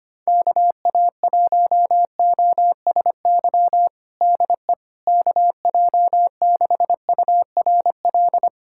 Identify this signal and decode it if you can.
Morse code, heard as a keyed tone, about 25 wpm, 700 Hz. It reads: KA1OSY DE KJ6URL
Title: USA Callsign DE Callsign - 6 Characters - 25wpm